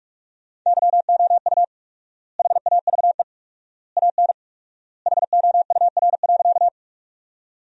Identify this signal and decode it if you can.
Morse code, heard as a keyed tone, about 46 wpm, 700 Hz. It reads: YOU HAVE AN HOUR.
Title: 4x Word-Spacing - Sentences from Top 400 Words - 45wpm